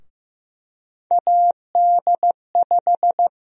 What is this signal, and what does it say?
Morse code, heard as a keyed tone, about 15 wpm, 700 Hz. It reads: AD5
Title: USA Call Sign Prefixes - Repeated 2x - 15wpm